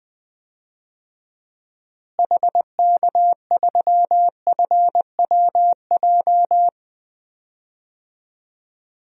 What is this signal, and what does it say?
Morse code, heard as a keyed tone, about 21 wpm, 700 Hz. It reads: HK3FWJ
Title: DXCC-100 Call Signs - 6 Characters - Repeated 2x - 20wpm